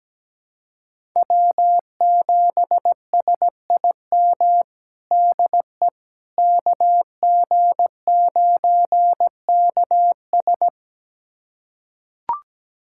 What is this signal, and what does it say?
Morse code, heard as a keyed tone, about 17 wpm, 700 Hz. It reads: W7SIM DE KG9KS
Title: USA Callsign DE Callsign - 5 Characters - 17wpm